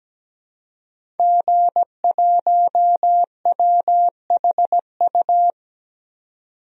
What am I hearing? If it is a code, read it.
Morse code, heard as a keyed tone, about 17 wpm, 700 Hz. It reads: G1WHU
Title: DXCC-100 Call Signs - 5 Characters - 17wpm